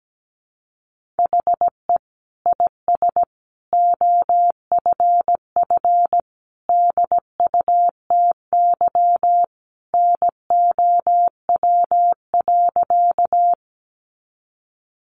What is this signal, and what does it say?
Morse code, heard as a keyed tone, about 17 wpm, 700 Hz. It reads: HE IS OFF DUTY NOW.
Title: Sentences from Top 1500 Words 17wpm